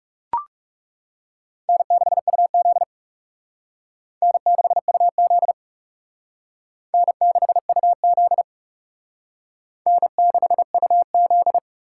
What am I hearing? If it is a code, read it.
Morse code, heard as a keyed tone, about 40 wpm, 700 Hz. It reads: N6UZ N6UZ N6UZ N6UZ